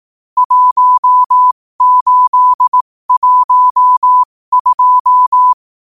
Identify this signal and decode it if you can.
Morse code, heard as a keyed tone, about 18 wpm, spaced as for 15 wpm, 1000 Hz. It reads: 1812
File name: qotdfiles / 15wpm